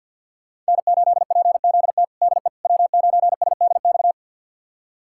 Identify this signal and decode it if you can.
Morse code, heard as a keyed tone, about 50 wpm, 700 Hz. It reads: N8PZT DE W9SDX
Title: USA Callsign DE Callsign - 5 Characters - 50wpm